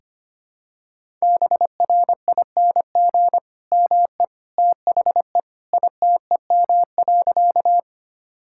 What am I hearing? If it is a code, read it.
Morse code, heard as a keyed tone, about 25 wpm, 700 Hz. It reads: BRING ME THE ITEM.
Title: Sentences from Top 1000 Words - Review Part 6 - 25wpm